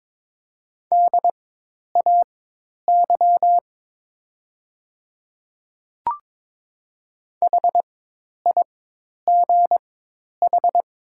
Morse code, heard as a keyed tone, about 22 wpm, spaced as for 10 wpm, 700 Hz. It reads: DAY HIGH
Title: 4x Character Spacing - Top 300 Words - 22wpm